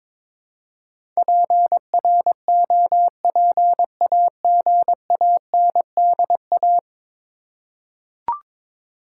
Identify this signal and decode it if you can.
Morse code, heard as a keyed tone, about 22 wpm, 700 Hz. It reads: PROPAGANDA
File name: practice / 22wpm / 10 Letter Words - First 3 Letters Given - 22wpm